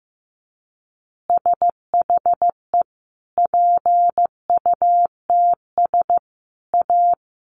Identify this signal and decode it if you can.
Morse code, heard as a keyed tone, about 15 wpm, 700 Hz. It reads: SHE PUTS A
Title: Easy Phrases 15wpm